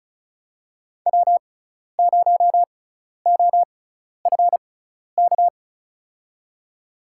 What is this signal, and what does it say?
Morse code, heard as a keyed tone, about 34 wpm, spaced as for 12 wpm, 700 Hz. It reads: W0OFK
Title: USA Call Signs - 6x Character Spacing - 35wpm